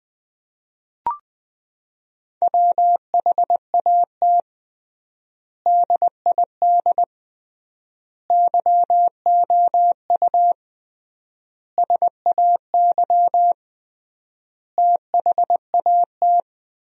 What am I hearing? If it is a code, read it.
Morse code, heard as a keyed tone, about 21 wpm, 700 Hz. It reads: WHAT DID YOU SAY THAT
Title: 3x Word-Spacing - Sentences from Top 200 Words - 20wpm